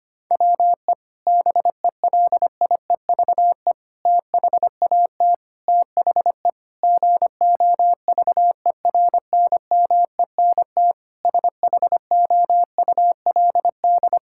Morse code, heard as a keyed tone, about 25 wpm, 700 Hz. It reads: WE BELIEVE THAT THE GOVERNMENT SHOULD
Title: Sentences from Top 1000 Words - Review Part 6 - 25wpm